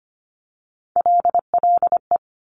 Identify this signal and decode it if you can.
Morse code, heard as a keyed tone, about 25 wpm, 700 Hz. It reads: LLE
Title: Top Three Letter Combinations in Words 25wpm